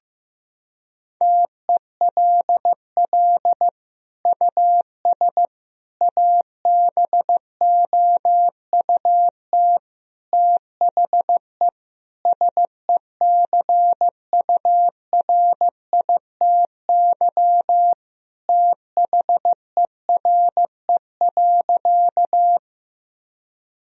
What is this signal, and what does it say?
Morse code, heard as a keyed tone, about 16 wpm, 700 Hz. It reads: TELL US ABOUT THE SECURITY THERE.
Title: Sentences from Top 1000 Words - Review Part 4 - 15wpm